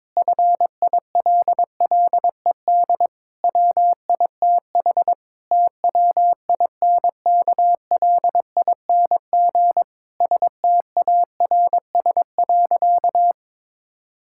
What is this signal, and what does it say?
Morse code, heard as a keyed tone, about 22 wpm, 700 Hz. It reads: FILLED WITH TWINKLING STARS.
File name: Difficult Sentences with Hints - 22wpm